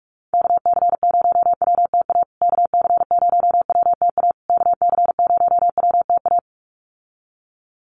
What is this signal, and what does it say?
Morse code, heard as a keyed tone, about 44 wpm, 700 Hz. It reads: KC0WTA KC0WTA KC0WTA
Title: USA Call Signs Repeated 3x 45wpm